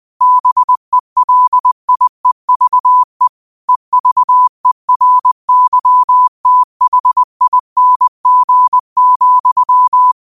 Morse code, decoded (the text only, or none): BELIEVE EVERYTHING,